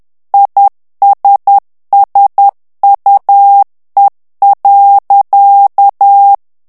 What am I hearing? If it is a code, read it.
Morse code, heard as a keyed tone, about 10 wpm, 800 Hz. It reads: ISSUE.
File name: morse mp3 / ANGLAIS texte clair 2 vitesse 10 mots minute